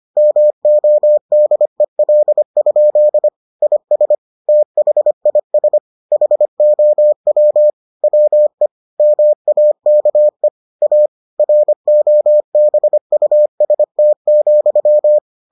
Morse code, heard as a keyed tone, about 25 wpm, 600 Hz. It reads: MODEL? IS THIS HOW WE MAKE A ROBUST,